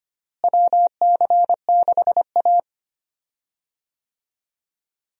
Morse code, heard as a keyed tone, about 25 wpm, 700 Hz. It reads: WC6A
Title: USA Call Signs - 4 Characters - Repeated 3x - 25wpm